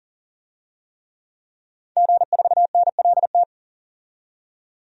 Morse code, heard as a keyed tone, about 40 wpm, 700 Hz. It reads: G4NLT
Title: DXCC-100 Call Signs - 5 Characters - 40wpm